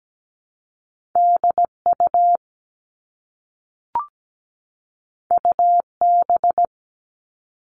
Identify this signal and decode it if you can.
Morse code, heard as a keyed tone, about 17 wpm, 700 Hz. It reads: DU UB
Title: All Cut Numbers - 2-Digit 17wpm